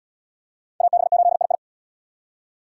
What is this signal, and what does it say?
Morse code, heard as a keyed tone, about 62 wpm, 700 Hz. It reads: AB8II
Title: Speed Racing - USA 2x2 Call Signs 50wpm